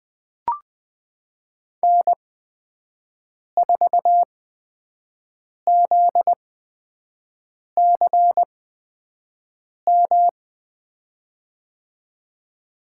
Morse code, heard as a keyed tone, about 20 wpm, spaced as for 5 wpm, 700 Hz. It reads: N4ZCM
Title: USA Call Signs - 5 Characters - 8x Character Spacing - 20wpm